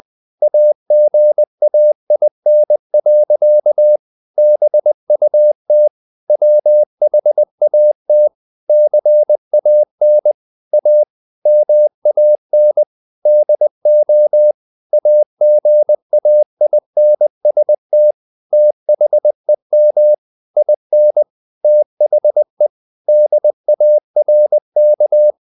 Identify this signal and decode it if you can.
Morse code, heard as a keyed tone, about 20 wpm, 600 Hz. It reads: AGAIN. BUT WHAT CAN A MAN DO AGAINST THEM IN THE DARK
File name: the old man and the sea part 2 20wpm